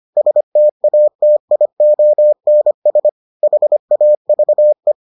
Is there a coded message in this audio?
Yes: Morse code from 0.2 to 4.9 s, heard as a keyed tone, about 25 wpm, 600 Hz. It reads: STATIONS HAVE